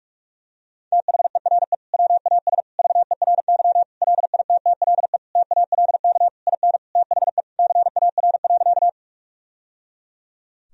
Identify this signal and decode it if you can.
Morse code, heard as a keyed tone, about 46 wpm, 700 Hz. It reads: THERE WAS VERY LITTLE TALK IN THE CAR.
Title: Sentences from Top 1000 Words - Review Part 1 - 45wpm